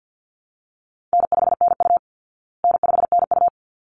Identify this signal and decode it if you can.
Morse code, heard as a keyed tone, about 50 wpm, 700 Hz. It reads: N5NU N5NU
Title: Easy Call Signs Repeated 2x - 50wpm